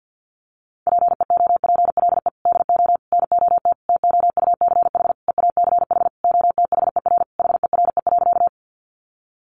Morse code, heard as a keyed tone, about 50 wpm, 700 Hz. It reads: PEOPLE DO NOT TOUCH EACH OTHER HERE.
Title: Sentences from Top 1200 Words 50wpm